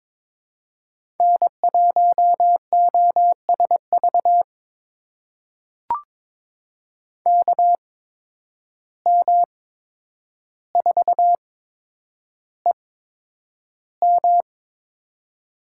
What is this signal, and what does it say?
Morse code, heard as a keyed tone, about 22 wpm, spaced as for 10 wpm, 700 Hz. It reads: N1OSV KM4EM
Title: USA Call Signs - 5 Characters - 8x Character Spacing - 22wpm